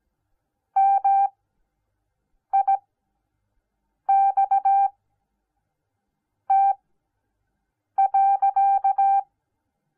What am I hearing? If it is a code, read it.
Morse code, heard as a keyed tone, about 17 wpm, spaced as for 6 wpm, 780 Hz. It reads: MIXT.